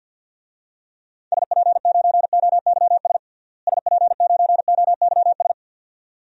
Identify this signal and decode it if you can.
Morse code, heard as a keyed tone, about 51 wpm, 700 Hz. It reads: SP9OYS SP9OYS
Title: DXCC-100 Call Signs - 6 Characters - Repeated 2x - 50wpm